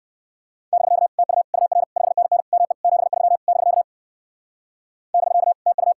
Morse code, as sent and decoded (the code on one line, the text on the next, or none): -...-.- / - ..- / ..- .-. / ..... -. -. / -- . / --... ...-- / -...-.- / -...-.- / - ..-
<BK> TU UR 5NN ME 73 <BK> <BK> TU